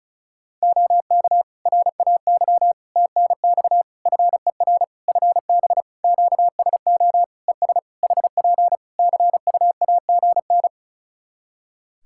Morse code, heard as a keyed tone, about 34 wpm, 700 Hz. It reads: OK RAY TNX FER FB QSO ES HP CUAGN
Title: Practice QSO - Exchange 4 - Ending - 35wpm